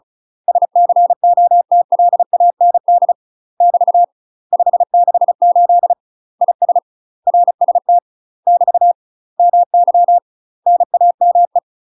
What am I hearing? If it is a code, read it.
Morse code, heard as a keyed tone, about 34 wpm, 700 Hz. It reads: SCOTLAND <BT> 568 IS RST <BT> MY NAME
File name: UK MASTER QSOs TRAINING / 35WPM / UK MASTER QSOs 0037